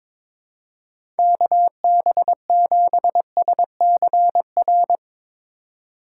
Morse code, heard as a keyed tone, about 22 wpm, 700 Hz. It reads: KB7SCR
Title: USA Call Signs - 3x Character Spacing - 22wpm